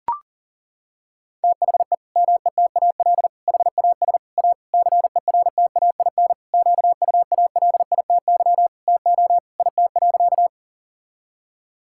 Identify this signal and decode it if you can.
Morse code, heard as a keyed tone, about 40 wpm, 700 Hz. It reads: THE METAL HAS A CERTAIN QUALITY TO IT.
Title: Sentences from Top 1000 Words - Review Part 6 - 40wpm